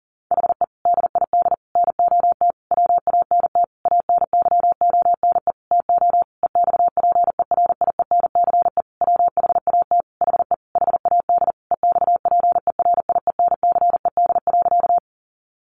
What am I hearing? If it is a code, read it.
Morse code, heard as a keyed tone, about 40 wpm, 700 Hz. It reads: HE DID NOT WANT ANYONE TO EXPERIENCE WHAT HE HAD EXPERIENCED.
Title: Sentences from Top 1100 Words 40wpm